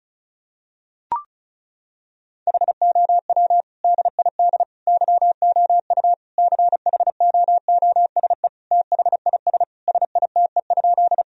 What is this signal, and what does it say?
Morse code, heard as a keyed tone, about 34 wpm, 700 Hz. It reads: HOW DID YOU CHOOSE THIS SITE?
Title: Sentences from Top 1000 Words - Review Part 4 - 35wpm